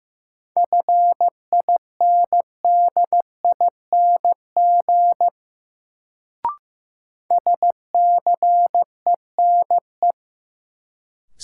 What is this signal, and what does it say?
Morse code, heard as a keyed tone, about 15 wpm, 700 Hz. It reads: FINDING SCENE
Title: Sentences from Top 1100 Words - New Word Drill - 15wpm